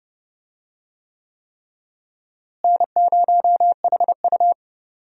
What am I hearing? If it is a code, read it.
Morse code, heard as a keyed tone, about 30 wpm, 700 Hz. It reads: N0HU
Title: USA Call Signs - 4 Characters - Repeated 3x - 30wpm